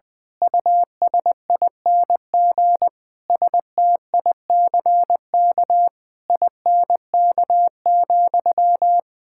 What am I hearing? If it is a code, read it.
Morse code, heard as a keyed tone, about 20 wpm, 700 Hz. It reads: USING STICK INK,
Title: Pangram Challenge v2 - 20wpm